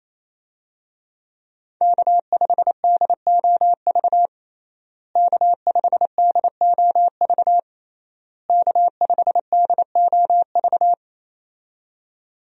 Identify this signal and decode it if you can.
Morse code, heard as a keyed tone, about 28 wpm, 700 Hz. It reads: K5DOV K5DOV K5DOV
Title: USA Call Signs - 5 Characters - Repeated 3x - 28wpm